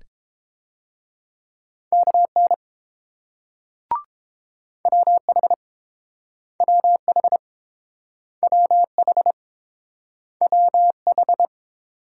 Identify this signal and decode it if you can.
Morse code, heard as a keyed tone, about 29 wpm, 700 Hz. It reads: KN WH WH WH WH